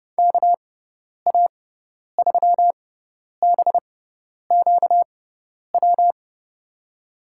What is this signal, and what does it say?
Morse code, heard as a keyed tone, about 31 wpm, spaced as for 10 wpm, 700 Hz. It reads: KA3BQW
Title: USA Call Signs - 6 Characters - 6x Character Spacing - 30wpm